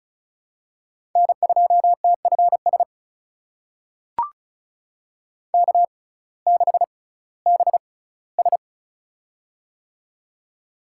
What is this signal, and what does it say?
Morse code, heard as a keyed tone, about 34 wpm, spaced as for 19 wpm, 700 Hz. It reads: N2TFS K6BS
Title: USA Call Signs - 6x Character Spacing - 35wpm